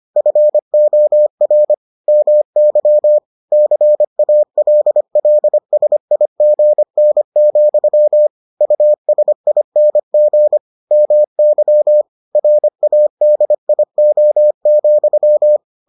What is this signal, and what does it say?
Morse code, heard as a keyed tone, about 25 wpm, 600 Hz. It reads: FOR MY CALLSIGN, USING MY RADIO,